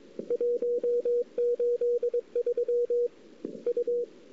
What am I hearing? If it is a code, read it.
Morse code, heard as a keyed tone, about 22 wpm, 480 Hz. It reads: 183 U
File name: CW / 2008-ss-cw-k5tr-op-n5rz -02:00